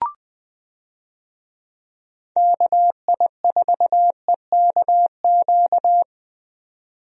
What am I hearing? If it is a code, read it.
Morse code, heard as a keyed tone, about 20 wpm, 700 Hz. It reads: KI4EKQ
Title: USA Call Signs - 6 Characters - 20wpm